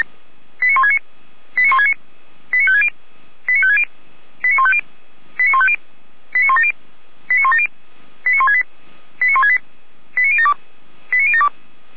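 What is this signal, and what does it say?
Five-tone selective call (ZVEI), heard as quick runs of five tones, tones of about 70 ms. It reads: 89169 89169 89560 89560 89150 89150 89170 89170 89168 89168 89072 89072
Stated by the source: ZVEI five-tone call received by SDR, from feuerwehr-orlamuende.org (Orlamünde fire brigade).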